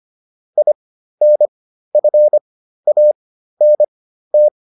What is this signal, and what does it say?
Morse code, heard as a keyed tone, about 25 wpm, spaced as for 13 wpm, 600 Hz. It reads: INFANT